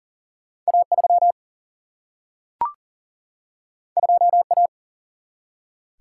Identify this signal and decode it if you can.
Morse code, heard as a keyed tone, about 39 wpm, 700 Hz. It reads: A3 2A
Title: Single-Letter Single-Number 40wpm